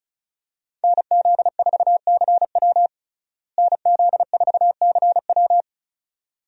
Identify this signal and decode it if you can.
Morse code, heard as a keyed tone, about 36 wpm, 700 Hz. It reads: NZ4CW NZ4CW
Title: USA Call Signs - 5 Characters - Repeated 2x - 35wpm